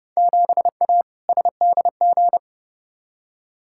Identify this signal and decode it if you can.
Morse code, heard as a keyed tone, about 32 wpm, 700 Hz. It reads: 7A SDG
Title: Field Day Exchanges Encore - 30wpm